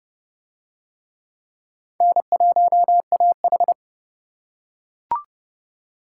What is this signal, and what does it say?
Morse code, heard as a keyed tone, about 29 wpm, 700 Hz. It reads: N1AH